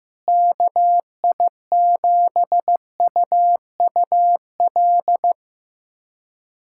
Morse code, heard as a keyed tone, about 15 wpm, 700 Hz. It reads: KI7UUL
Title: USA Callsign DE Callsign 15wpm